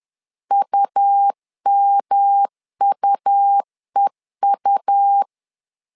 Morse code, heard as a keyed tone, about 11 wpm, 800 Hz. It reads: UMUEU
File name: LEZIONI / 3 MWRO / 3 M 10wpm 1828